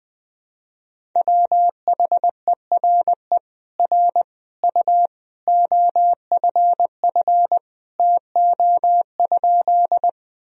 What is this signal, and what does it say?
Morse code, heard as a keyed tone, about 20 wpm, 700 Hz. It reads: WHERE R U OFF TO?
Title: Sentences from Top 200 Words and Abbreviations 20wpm